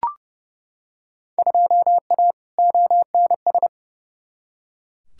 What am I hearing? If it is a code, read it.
Morse code, heard as a keyed tone, about 32 wpm, 700 Hz. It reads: 2A ONS
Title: Field Day Exchanges Encore - 30wpm